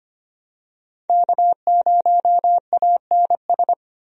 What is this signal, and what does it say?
Morse code, heard as a keyed tone, about 25 wpm, 700 Hz. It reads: K0ANS
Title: USA Call Signs - 5 Characters - 25wpm